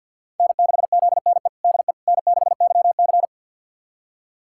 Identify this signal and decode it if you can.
Morse code, heard as a keyed tone, about 46 wpm, 700 Hz. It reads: N6ZNE DE N6YC
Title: USA Callsign DE Callsign 50wpm